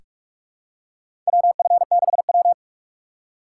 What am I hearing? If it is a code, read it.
Morse code, heard as a keyed tone, about 42 wpm, 700 Hz. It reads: WF6W
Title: USA Call Signs - 4x Character Spacing - 45wpm